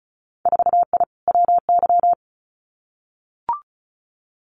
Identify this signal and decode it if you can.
Morse code, heard as a keyed tone, about 36 wpm, 700 Hz. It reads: 4I WY
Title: Winter Field Day Exchanges Encore 35wpm